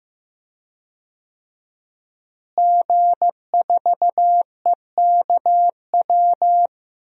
Morse code, heard as a keyed tone, about 15 wpm, 700 Hz. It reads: G4EKW
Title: DXCC-100 Call Signs - 5 Characters - Repeated 3x - 15wpm